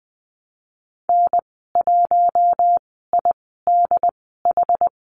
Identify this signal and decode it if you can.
Morse code, heard as a keyed tone, about 19 wpm, spaced as for 14 wpm, 700 Hz. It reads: N1IDH
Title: 069 Lesson 12g - Letter d - Call Signs 20wpm